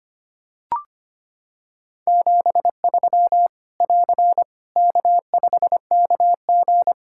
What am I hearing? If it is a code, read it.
Morse code, heard as a keyed tone, about 25 wpm, 700 Hz. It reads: 73 <AR> K5KG
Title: K1USN SST QSO Practice 25wpm